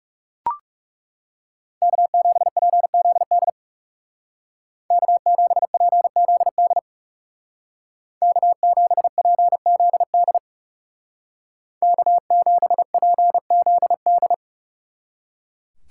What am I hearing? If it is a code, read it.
Morse code, heard as a keyed tone, about 36 wpm, 700 Hz. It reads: K7PZD K7PZD K7PZD K7PZD
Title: Speed Racing - USA 1x3 Call Signs 30wpm